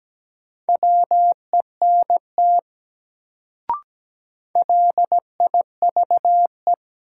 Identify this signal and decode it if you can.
Morse code, heard as a keyed tone, about 17 wpm, 700 Hz. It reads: WENT LIVE
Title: Sentences from Top 200 Words - New Word Drill - 17wpm